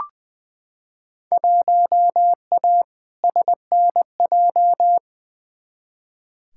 Morse code, heard as a keyed tone, about 20 wpm, 700 Hz. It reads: 1A SNJ
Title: Field Day Exchanges - 20wpm